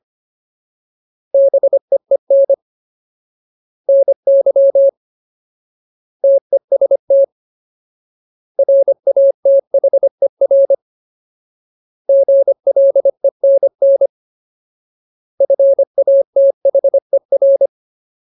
Morse code, heard as a keyed tone, about 25 wpm, 550 Hz. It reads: BEEN NY TEST RATHER GLENN FATHER
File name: Ultimate Word List / 25WPM Ultimate